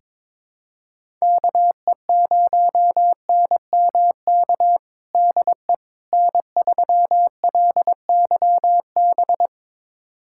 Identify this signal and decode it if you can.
Morse code, heard as a keyed tone, about 22 wpm, 700 Hz. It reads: KE0NMK DE N3LYB
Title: USA Callsign DE Callsign 22wpm